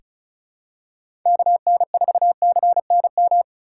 Morse code, heard as a keyed tone, about 34 wpm, 700 Hz. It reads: KN4CNM
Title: USA Call Signs - 6 Characters - 2x Character Spacing - 35wpm